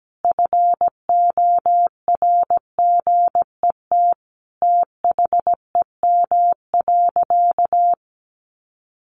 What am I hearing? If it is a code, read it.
Morse code, heard as a keyed tone, about 17 wpm, 700 Hz. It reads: FORGET THEM.
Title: Sentences from Top 1200 Words 17wpm